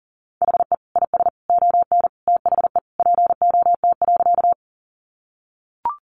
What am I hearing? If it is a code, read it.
Morse code, heard as a keyed tone, about 40 wpm, 700 Hz. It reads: HE IS ON THE POT.
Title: Sentences from Top 1000 Words 40wpm